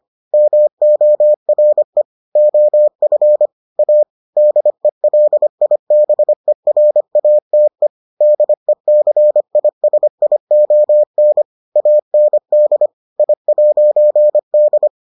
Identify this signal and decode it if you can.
Morse code, heard as a keyed tone, about 25 wpm, 600 Hz. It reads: MORE OF A DELIBERATE DECISION AND I'D